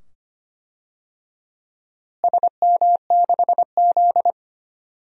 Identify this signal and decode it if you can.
Morse code, heard as a keyed tone, about 25 wpm, 700 Hz. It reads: SM6Z
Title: DXCC-100 Call Signs - 4 Characters - Repeated 2x - 25wpm